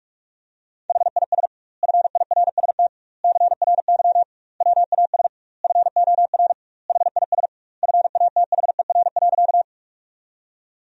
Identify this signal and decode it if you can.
Morse code, heard as a keyed tone, about 48 wpm, 700 Hz. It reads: HIS FIRST CRY WAS FOR HIS FATHER.
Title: 2x Word-Spacing - Sentences from Top 500 Words - 45wpm